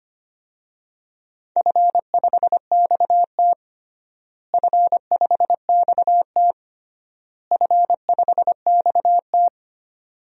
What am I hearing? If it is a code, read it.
Morse code, heard as a keyed tone, about 25 wpm, 700 Hz. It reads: F5XT F5XT F5XT
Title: DXCC-100 Call Signs - 4 Characters - Repeated 3x - 25wpm